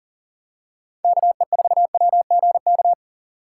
Morse code, heard as a keyed tone, about 40 wpm, 700 Hz. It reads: KE4WGK
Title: USA Call Signs - 6 Characters - 4x Character Spacing - 40wpm